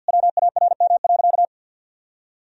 Morse code, heard as a keyed tone, about 49 wpm, 700 Hz. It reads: WARM.
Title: Sentences from Top 1000 Words - Review Part 2 - 50wpm